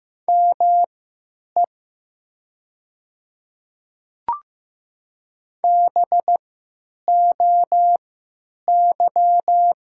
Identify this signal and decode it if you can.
Morse code, heard as a keyed tone, about 15 wpm, spaced as for 9 wpm, 700 Hz. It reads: ME BOY
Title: 3x Character Spacing - Top 200 Words - 15wpm